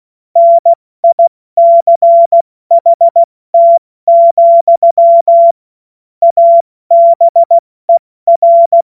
Morse code, heard as a keyed tone, about 16 wpm, 680 Hz. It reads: NICHT, ABER
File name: Morsekurs / Witze / Witz 27. Juni 2023 70 F80